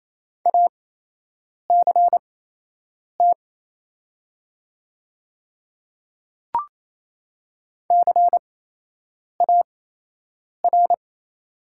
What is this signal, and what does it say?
Morse code, heard as a keyed tone, about 27 wpm, spaced as for 8 wpm, 700 Hz. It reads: ACT CAR
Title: 8x Character Spacing - Top 300 Words - 28wpm